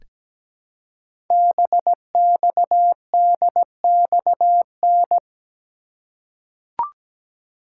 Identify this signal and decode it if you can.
Morse code, heard as a keyed tone, about 17 wpm, 700 Hz. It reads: BXDXN
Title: Code Groups - 5 Letter:Numbers - 17wpm